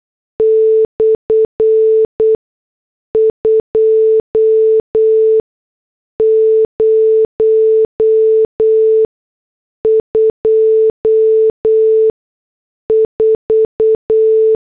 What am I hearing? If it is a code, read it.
Morse code, heard as a keyed tone, about 8 wpm, 440 Hz. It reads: / 2 0 2 4
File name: morse / headlines